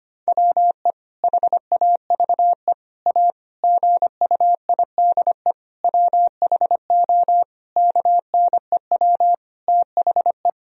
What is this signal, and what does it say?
Morse code, heard as a keyed tone, about 25 wpm, 700 Hz. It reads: WE HAVE A GUIDE WHO KNEW THE
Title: Sentences from Top 1000 Words - Review Part 3 - 25wpm